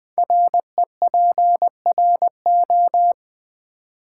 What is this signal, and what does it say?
Morse code, heard as a keyed tone, about 20 wpm, 700 Hz. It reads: REPRO
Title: Build A Word - 5 Letters At A Time - 12-Letter Words - 20wpm